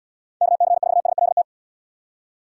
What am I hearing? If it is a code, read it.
Morse code, heard as a keyed tone, about 70 wpm, 700 Hz. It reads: KC4SLT